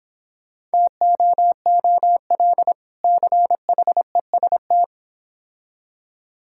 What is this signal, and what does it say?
Morse code, heard as a keyed tone, about 26 wpm, 700 Hz. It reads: TOOL CHEST